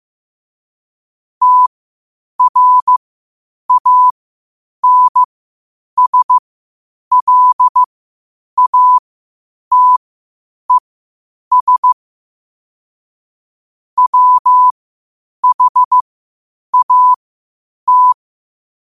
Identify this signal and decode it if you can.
Morse code, heard as a keyed tone, about 15 wpm, spaced as for 8 wpm, 1000 Hz. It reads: TRANSLATES WHAT
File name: qotdfiles / 8wpm